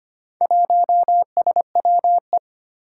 Morse code, heard as a keyed tone, about 25 wpm, 700 Hz. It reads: 1SWE
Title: Four Challenging Letter-Number Combinations 25wpm